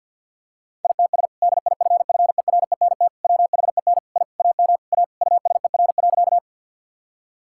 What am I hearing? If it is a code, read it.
Morse code, heard as a keyed tone, about 50 wpm, 700 Hz. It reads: ITS DIFFERENT WHEN I AM A USER.
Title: Sentences from Top 1000 Words - Review Part 5 - 50wpm